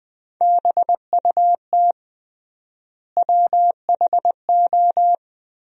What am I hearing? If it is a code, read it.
Morse code, heard as a keyed tone, about 20 wpm, 700 Hz. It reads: BUT WHO
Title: 3x Word-Spacing - Sentences from Top 100 Words - 20wpm